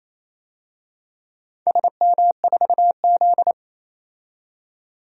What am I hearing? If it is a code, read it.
Morse code, heard as a keyed tone, about 28 wpm, 700 Hz. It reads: SM4Z
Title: DXCC-100 Call Signs - 4 Characters - 28wpm